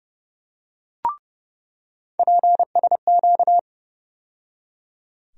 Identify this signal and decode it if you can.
Morse code, heard as a keyed tone, about 31 wpm, 700 Hz. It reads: PSQ
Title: USA Call Sign Postfixes - 3 Letters 30wpm